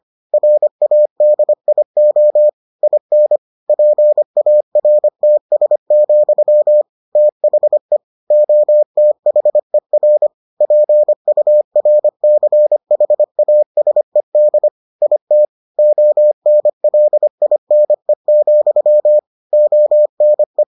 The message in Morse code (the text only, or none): RADIO IN PARTS, THE OTHER PURCHASED IT ONLINE, ONE